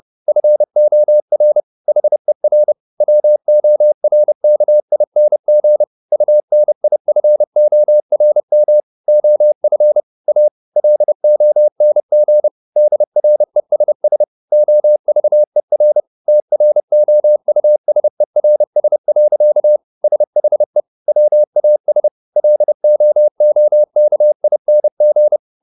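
Morse code, heard as a keyed tone, about 30 wpm, 600 Hz. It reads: FOR HER WORKING UNIFORM OF A LONG DRESS OVER TROUSERS. SHE WAS LOOKING
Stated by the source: Morse code at 30 wpm, the novel Never by Ken Follett in English, from CW med Gnister.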